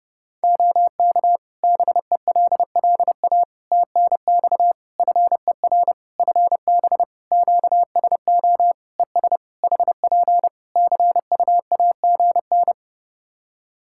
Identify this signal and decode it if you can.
Morse code, heard as a keyed tone, about 30 wpm, 700 Hz. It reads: OK BELLA TNX FER FB QSO ES HP CUAGN
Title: K1USN SST QSO Practice 30wpm